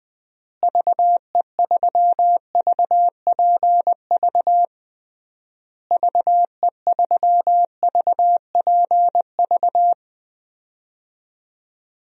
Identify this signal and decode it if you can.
Morse code, heard as a keyed tone, about 21 wpm, 700 Hz. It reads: VE3VPV VE3VPV
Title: DXCC-100 Call Signs - 6 Characters - Repeated 3x - 20wpm